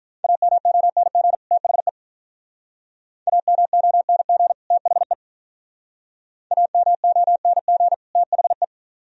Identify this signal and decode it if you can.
Morse code, heard as a keyed tone, about 44 wpm, 700 Hz. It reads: AMONG THE AMONG THE AMONG THE
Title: Speed-Racing - Sets of 2 Words 35wpm